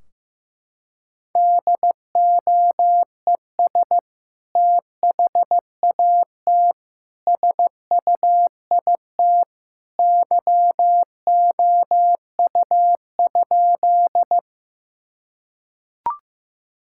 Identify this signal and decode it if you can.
Morse code, heard as a keyed tone, about 14 wpm, 700 Hz. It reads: DOES THAT SUIT YOU?
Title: Sentences from Top 1600 Words 15wpm